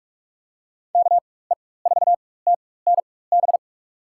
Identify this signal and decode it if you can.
Morse code, heard as a keyed tone, about 46 wpm, spaced as for 21 wpm, 700 Hz. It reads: KE4TNB
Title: USA Call Signs - 6 Characters - 4x Character Spacing - 45wpm